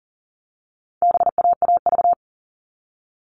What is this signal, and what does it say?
Morse code, heard as a keyed tone, about 40 wpm, 700 Hz. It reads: BAAV
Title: All Cut Numbers - 4-Digit 40wpm